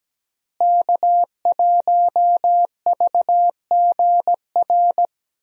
Morse code, heard as a keyed tone, about 17 wpm, 700 Hz. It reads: K1VGR
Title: USA Call Signs - 5 Characters - Repeated 2x - 17wpm